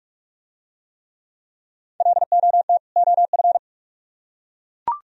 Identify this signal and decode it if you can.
Morse code, heard as a keyed tone, about 42 wpm, 700 Hz. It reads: LOT OF